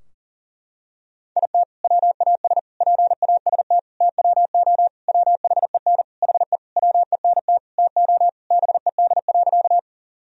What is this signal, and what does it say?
Morse code, heard as a keyed tone, about 40 wpm, 700 Hz. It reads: IT WAS PAST TWO WHEN HE WENT TO BED.
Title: Sentences from Top 1000 Words - Review Part 3 - 40wpm